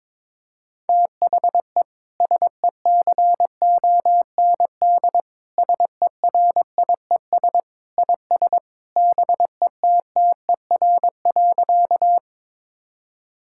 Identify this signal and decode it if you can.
Morse code, heard as a keyed tone, about 22 wpm, 700 Hz. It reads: THE SECOND SERIES IS BETTER.
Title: Sentences from Top 1000 Words - Review Part 5 - 22wpm